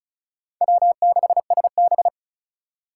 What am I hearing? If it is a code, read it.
Morse code, heard as a keyed tone, about 34 wpm, 700 Hz. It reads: W6SB
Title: USA Call Signs - 4 Characters - 4x Character Spacing - 35wpm